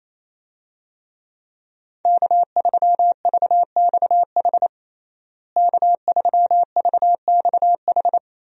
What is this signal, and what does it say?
Morse code, heard as a keyed tone, about 28 wpm, 700 Hz. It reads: K3VXH K3VXH
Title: USA Call Signs - 5 Characters - Repeated 2x - 28wpm